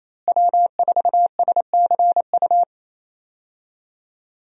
Morse code, heard as a keyed tone, about 28 wpm, 700 Hz. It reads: W4SCU
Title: USA Call Signs - 5 Characters - Repeated 2x - 28wpm